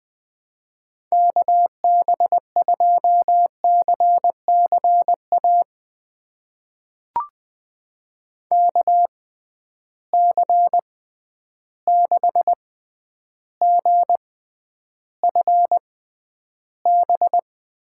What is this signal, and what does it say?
Morse code, heard as a keyed tone, about 20 wpm, spaced as for 10 wpm, 700 Hz. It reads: KB2CCA KC6GFB
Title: USA Call Signs - 6 Characters - 6x Character Spacing - 20wpm